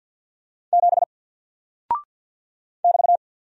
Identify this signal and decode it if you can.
Morse code, heard as a keyed tone, about 50 wpm, 700 Hz. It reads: 7 <BT>
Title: Single Character (Letter:Number:Punctuation:Prosign) 50wpm